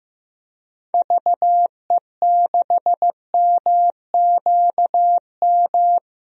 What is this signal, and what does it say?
Morse code, heard as a keyed tone, about 15 wpm, 700 Hz. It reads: VE6MQM
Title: DXCC-100 Call Signs - 6 Characters - Repeated 2x - 15wpm